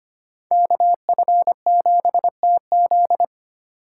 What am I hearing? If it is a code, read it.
Morse code, heard as a keyed tone, about 32 wpm, 700 Hz. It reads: KF7TZ